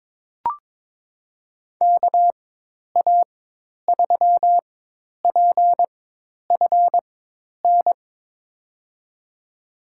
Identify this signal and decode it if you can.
Morse code, heard as a keyed tone, about 22 wpm, spaced as for 10 wpm, 700 Hz. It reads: KA3PFN
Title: USA Call Signs - 6 Characters - 4x Character Spacing - 22wpm